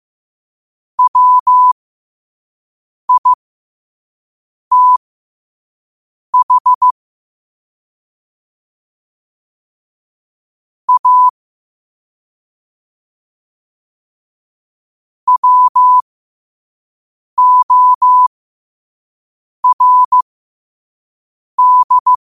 Morse code, heard as a keyed tone, about 15 wpm, spaced as for 5 wpm, 1000 Hz. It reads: WITH A WORD
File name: qotdfiles / 5wpm